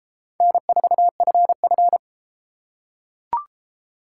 Morse code, heard as a keyed tone, about 33 wpm, 700 Hz. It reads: N4FF